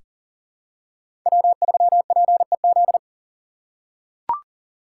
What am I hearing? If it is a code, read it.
Morse code, heard as a keyed tone, about 40 wpm, 700 Hz. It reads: W3PEZ
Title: USA Call Signs - 5 Characters - 3x Character Spacing - 40wpm